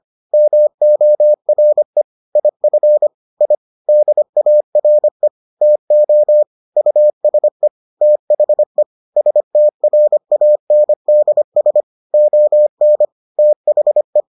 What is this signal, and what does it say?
Morse code, heard as a keyed tone, about 25 wpm, 600 Hz. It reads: MORE IF I DARE TO USE THE STRANDS ON THE